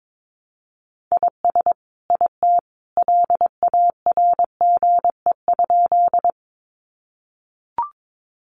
Morse code, heard as a keyed tone, about 22 wpm, 700 Hz. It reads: IS IT LARGE?
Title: Sentences from Top 200 Words 22wpm